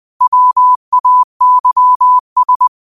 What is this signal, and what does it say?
Morse code, heard as a keyed tone, about 20 wpm, 1000 Hz. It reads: WAYS
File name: qotdfiles / 20wpm